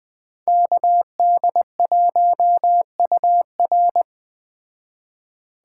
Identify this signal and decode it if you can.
Morse code, heard as a keyed tone, about 20 wpm, 700 Hz. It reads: KD1UR